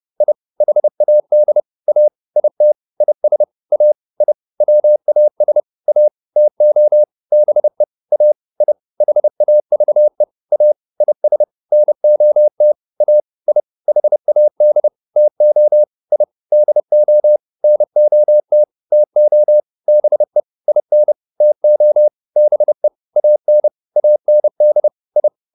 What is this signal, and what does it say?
Morse code, heard as a keyed tone, about 30 wpm, 600 Hz. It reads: I HAD A IT IS A I WAS A TO BE A I HAVE A IS NOT A I HAD TO I DO NOT TO BE IN TO BE AN AND I
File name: WD201 30